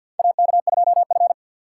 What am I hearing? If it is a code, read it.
Morse code, heard as a keyed tone, about 46 wpm, 700 Hz. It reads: AK2F
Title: USA Call Signs - 2x Character Spacing - 50wpm